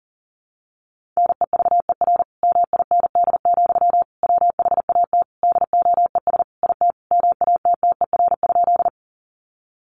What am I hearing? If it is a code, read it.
Morse code, heard as a keyed tone, about 40 wpm, 700 Hz. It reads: NEVER MIND, WHAT DOES IT MATTER?
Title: Sentences from Top 700 Words 40wpm